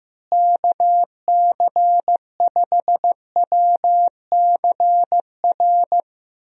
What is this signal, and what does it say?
Morse code, heard as a keyed tone, about 16 wpm, 700 Hz. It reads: KC5WCR
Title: USA Call Signs - 6 Characters - Repeated 2x - 15wpm